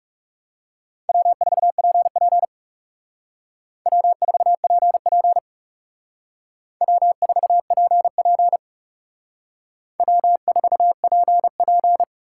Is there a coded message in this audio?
Yes: Morse code from 1.1 to 12.0 s, heard as a keyed tone, about 38 wpm, 700 Hz. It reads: W4PP W4PP W4PP W4PP